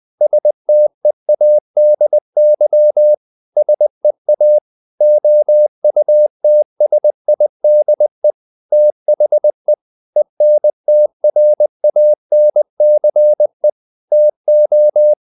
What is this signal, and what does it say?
Morse code, heard as a keyed tone, about 20 wpm, 600 Hz. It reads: STEADY SEA OUTSIDE THE ENTRANCE TO
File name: the old man and the sea part 2 20wpm